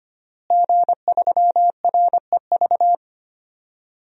Morse code, heard as a keyed tone, about 25 wpm, 700 Hz. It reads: G3REV
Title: DXCC-100 Call Signs - 5 Characters - Repeated 3x - 25wpm